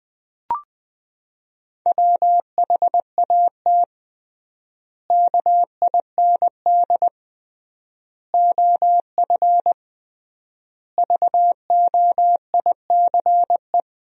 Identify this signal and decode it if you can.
Morse code, heard as a keyed tone, about 20 wpm, 700 Hz. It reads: WHAT KIND OF VOICE
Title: 3x Word-Spacing - Sentences from Top 500 Words - 20wpm